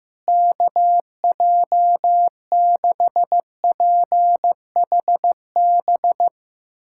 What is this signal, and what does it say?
Morse code, heard as a keyed tone, about 16 wpm, 700 Hz. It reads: KJ6PHB
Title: USA Callsign DE Callsign - 6 Characters - 15wpm